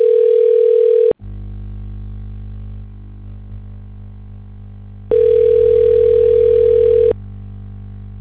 Ringback tone (North America), 440 + 480 Hz.